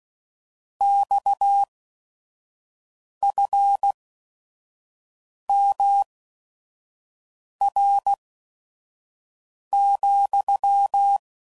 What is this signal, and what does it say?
Morse code, heard as a keyed tone, about 16 wpm, spaced as for 5 wpm, 800 Hz. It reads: XFMR,
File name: CW Code Groups @5WPM